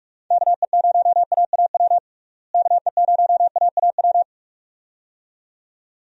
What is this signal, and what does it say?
Morse code, heard as a keyed tone, about 46 wpm, 700 Hz. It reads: KE0AAW KE0AAW
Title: USA Call Signs - 6 Characters - Repeated 2x - 45wpm